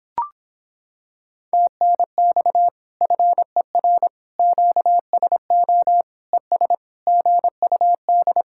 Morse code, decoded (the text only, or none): TNX FER QSO ES GUD